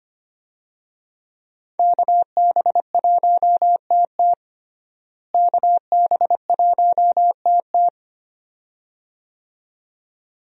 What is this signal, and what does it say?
Morse code, heard as a keyed tone, about 25 wpm, 690 Hz. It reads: KB1TT KB1TT
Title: USA Call Signs - 5 Characters - Repeated 2x - 25wpm